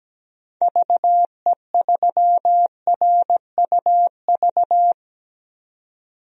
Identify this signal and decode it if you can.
Morse code, heard as a keyed tone, about 17 wpm, 700 Hz. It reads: VE3RUV
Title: DXCC-100 Call Signs - 6 Characters - 17wpm